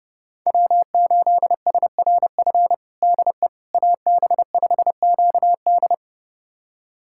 Morse code, heard as a keyed tone, about 32 wpm, 700 Hz. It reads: W8SRF DE AB5QD
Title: USA Callsign DE Callsign - 5 Characters - 30wpm